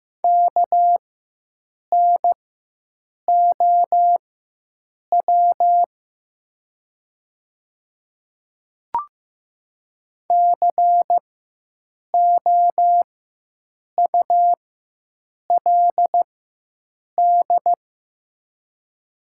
Morse code, heard as a keyed tone, about 15 wpm, spaced as for 7 wpm, 700 Hz. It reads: KNOW COULD
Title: 4x Character Spacing - Top 100 Words - 15wpm